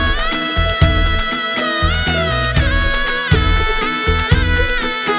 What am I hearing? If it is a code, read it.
Emergency Alert System (SAME) tones, heard as harsh buzzing bursts. It is heard from 1.0 to 4.8 s.